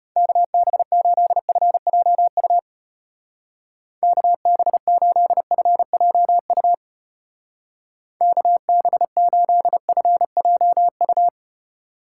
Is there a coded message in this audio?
Yes: Morse code from 0.2 to 11.3 s, heard as a keyed tone, about 32 wpm, 700 Hz. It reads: KB8FJU KB8FJU KB8FJU